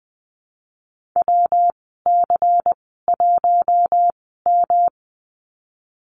Morse code, heard as a keyed tone, about 20 wpm, spaced as for 14 wpm, 700 Hz. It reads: WC1M
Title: 173 Lesson 28g - Number 9 - Call Signs 20wpm